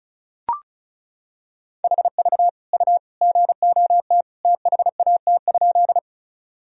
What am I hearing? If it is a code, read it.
Morse code, heard as a keyed tone, about 36 wpm, 700 Hz. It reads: HV U GOT THAT?
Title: Sentences from Top 300 Words and Abbreviations 35wpm